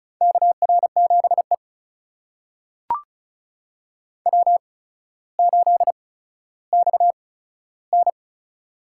Morse code, heard as a keyed tone, about 34 wpm, spaced as for 15 wpm, 700 Hz. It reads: KR7E W8XN
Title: USA Call Signs - 4 Characters - 8x Character Spacing - 35wpm